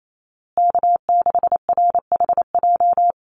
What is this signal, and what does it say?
Morse code, heard as a keyed tone, about 28 wpm, 700 Hz. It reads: K6RHJ
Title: USA Call Signs Repeated 3x 28wpm